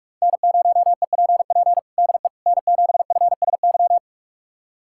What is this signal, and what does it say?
Morse code, heard as a keyed tone, about 46 wpm, 700 Hz. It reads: N0EPP DE N7FSY
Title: USA Callsign DE Callsign - 5 Characters - 45wpm